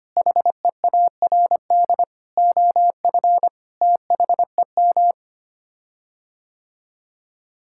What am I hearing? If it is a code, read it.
Morse code, heard as a keyed tone, about 25 wpm, 700 Hz. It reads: HEARD OF THEM
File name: Sets of 9 Words - 25wpm